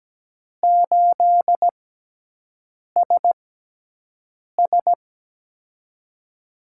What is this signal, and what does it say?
Morse code, heard as a keyed tone, about 17 wpm, spaced as for 6 wpm, 700 Hz. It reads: 8SS
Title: USA Call Signs - 4 Characters - 6x Character Spacing - 17wpm